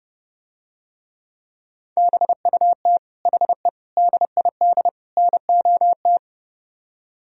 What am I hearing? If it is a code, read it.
Morse code, heard as a keyed tone, about 32 wpm, 700 Hz. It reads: BUT HE DID NOT